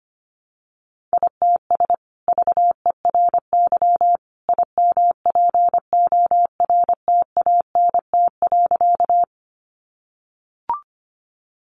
Morse code, heard as a keyed tone, about 25 wpm, 700 Hz. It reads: ITS VERY IMPORTANT.
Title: Sentences from Top 600 Words 25wpm